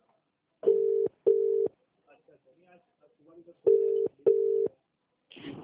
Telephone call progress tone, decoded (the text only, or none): ringback tone